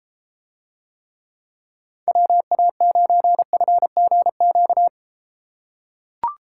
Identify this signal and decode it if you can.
Morse code, heard as a keyed tone, about 33 wpm, 700 Hz. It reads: WA9FGQ